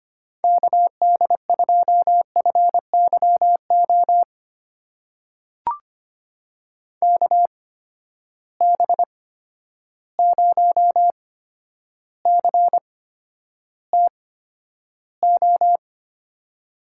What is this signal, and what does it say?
Morse code, heard as a keyed tone, about 25 wpm, spaced as for 11 wpm, 700 Hz. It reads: KD2FYO KB0CTO
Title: USA Call Signs - 6 Characters - 8x Character Spacing - 25wpm